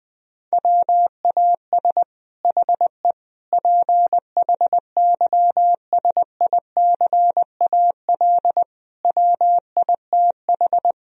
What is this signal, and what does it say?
Morse code, heard as a keyed tone, about 20 wpm, 700 Hz. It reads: WAS HE PHYSICAL WITH
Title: Sentences from Top 1000 Words - Review Part 4 - 20wpm